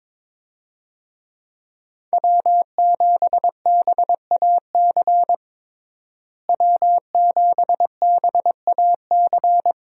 Morse code, heard as a keyed tone, about 22 wpm, 700 Hz. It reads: W7BAC W7BAC
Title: USA Call Signs - 5 Characters - Repeated 3x - 22wpm